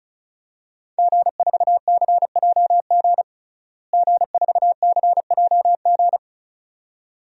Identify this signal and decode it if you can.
Morse code, heard as a keyed tone, about 34 wpm, 700 Hz. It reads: G4CJG G4CJG
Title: DXCC-100 Call Signs - 5 Characters - Repeated 2x - 35wpm